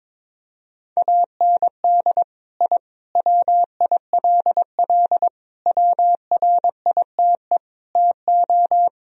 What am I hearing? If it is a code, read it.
Morse code, heard as a keyed tone, about 22 wpm, 700 Hz. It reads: AND I WILL WRITE TO
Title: Sentences from Top 1000 Words - Review Part 1 - 22wpm